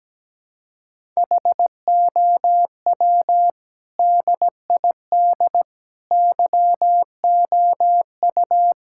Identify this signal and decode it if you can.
Morse code, heard as a keyed tone, about 17 wpm, 700 Hz. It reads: HOW DID YOU
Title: Sentences from Top 1000 Words - Review Part 5 - 17wpm